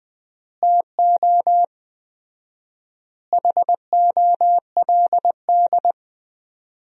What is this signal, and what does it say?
Morse code, heard as a keyed tone, about 20 wpm, 700 Hz. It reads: TO HOLD
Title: 4x Word-Spacing - Sentences from Top 400 Words - 20wpm